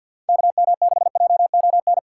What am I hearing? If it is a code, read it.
Morse code, heard as a keyed tone, about 49 wpm, 700 Hz. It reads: KM6JON
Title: USA Call Signs - 6 Characters - 3x Character Spacing - 50wpm